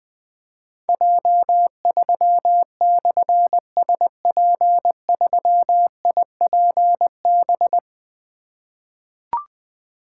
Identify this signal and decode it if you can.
Morse code, heard as a keyed tone, about 20 wpm, 700 Hz. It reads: J3/SP3IPB
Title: Difficult Call Signs 20wpm